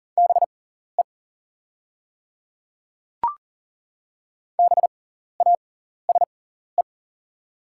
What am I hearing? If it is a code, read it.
Morse code, heard as a keyed tone, about 40 wpm, spaced as for 14 wpm, 690 Hz. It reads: BE BASE
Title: 6x Character Spacing - Top 300 Words - 40wpm